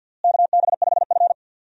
Morse code, heard as a keyed tone, about 47 wpm, 700 Hz. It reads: KB5F
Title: USA Call Signs - 4 Characters - Repeated 3x - 50wpm